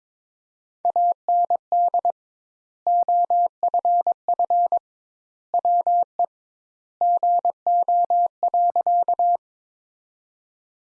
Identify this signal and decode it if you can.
Morse code, heard as a keyed tone, about 22 wpm, 700 Hz. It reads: AND OFF WE GO.
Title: Sentences Top 200 Words - 22wpm